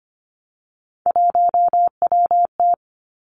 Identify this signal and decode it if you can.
Morse code, heard as a keyed tone, about 25 wpm, 700 Hz. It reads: 1WT
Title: Three Challenging Letter-Number Combinations 25wpm